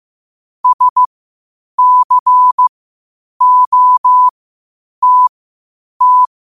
Morse code, heard as a keyed tone, about 15 wpm, spaced as for 8 wpm, 1000 Hz. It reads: SCOTT
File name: qotdfiles / 8wpm